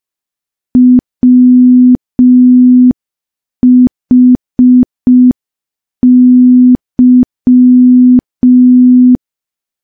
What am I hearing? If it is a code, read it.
Morse code, heard as a keyed tone, about 5 wpm, 250 Hz. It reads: WHY
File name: morse / morse-training-05WPM